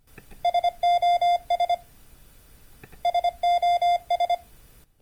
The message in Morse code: SOS SOS